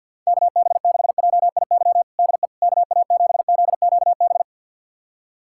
Morse code, heard as a keyed tone, about 46 wpm, 700 Hz. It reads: KB6JIY DE KA7ZQB
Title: USA Callsign DE Callsign - 6 Characters - 50wpm